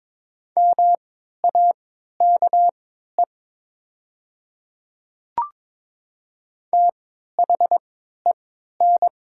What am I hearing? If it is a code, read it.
Morse code, heard as a keyed tone, about 22 wpm, spaced as for 12 wpm, 700 Hz. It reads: MAKE THEN
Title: 3x Character Spacing - Top 100 Words - 22wpm